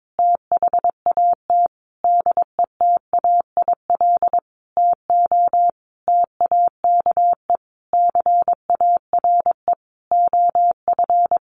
Morse code, heard as a keyed tone, about 22 wpm, 700 Hz. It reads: THAT DETAIL TO TAKE CARE OF
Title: Sentences from Top 1500 Words 22wpm